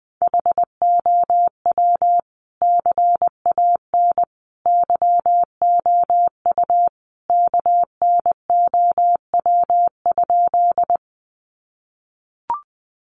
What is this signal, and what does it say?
Morse code, heard as a keyed tone, about 20 wpm, 700 Hz. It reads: HOW CAN YOU KNOW?
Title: Sentences from Top 100 Words 20wpm